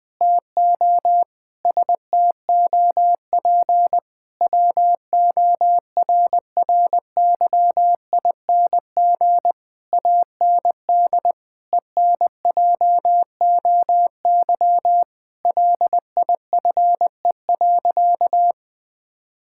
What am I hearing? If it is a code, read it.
Morse code, heard as a keyed tone, about 20 wpm, 700 Hz. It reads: TO STOP WORRYING AND ENJOY LIFE.
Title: Sentences with 8-Word Phrases - Spoken Beginning - 20wpm